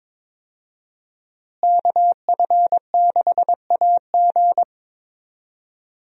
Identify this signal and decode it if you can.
Morse code, heard as a keyed tone, about 22 wpm, 700 Hz. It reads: KF6AG
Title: USA Call Signs - 5 Characters - Repeated 2x - 22wpm